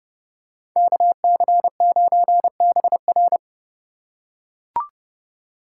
Morse code, heard as a keyed tone, about 29 wpm, 700 Hz. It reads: KC9BR